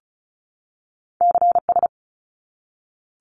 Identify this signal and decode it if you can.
Morse code, heard as a keyed tone, about 36 wpm, 700 Hz. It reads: CS
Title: Abbreviations 35wpm